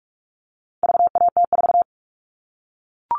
Morse code, heard as a keyed tone, about 44 wpm, 700 Hz. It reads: VAT4
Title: All Cut Numbers - 4-Digit 45wpm